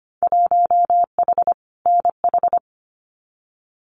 Morse code, heard as a keyed tone, about 25 wpm, 700 Hz. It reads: 1H NH
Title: Winter Field Day Exchanges 25wpm